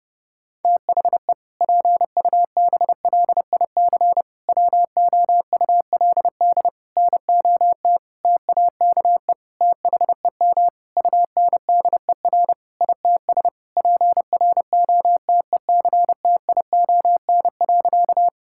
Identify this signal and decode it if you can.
Morse code, heard as a keyed tone, about 30 wpm, 700 Hz. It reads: THE PUBLIC WOULD NOT TAKE THEM UNDER ITS PROTECTION.
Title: Sentences from Top 1600 Words 30wpm